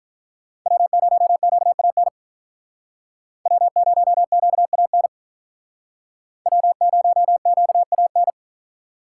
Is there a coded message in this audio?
Yes: Morse code from 0.7 to 8.3 s, heard as a keyed tone, about 47 wpm, 700 Hz. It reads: W0QAN W0QAN W0QAN